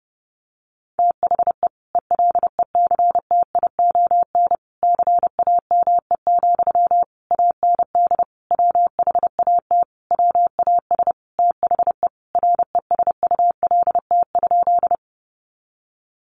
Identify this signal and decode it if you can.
Morse code, heard as a keyed tone, about 30 wpm, 700 Hz. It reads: THE ELECTION CAME, AND WHAT WAS THE RESULT?
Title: Sentences from Top 1500 Words 30wpm